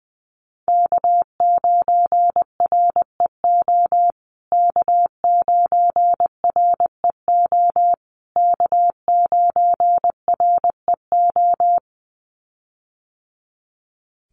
Morse code, heard as a keyed tone, about 20 wpm, 700 Hz. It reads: K9REO K9REO K9REO
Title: USA Call Signs Repeated 3x 20wpm